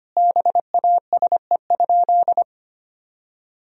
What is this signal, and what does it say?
Morse code, heard as a keyed tone, about 25 wpm, 700 Hz. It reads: BASE?
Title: Sentences from Top 1000 Words - Review Part 1 - 25wpm